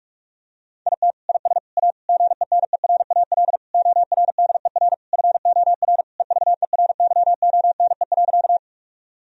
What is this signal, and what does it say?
Morse code, heard as a keyed tone, about 46 wpm, 700 Hz. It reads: IT IS A GENERAL ORDER FOR EVERYONE.
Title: Sentences from Top 1000 Words - Review Part 3 - 45wpm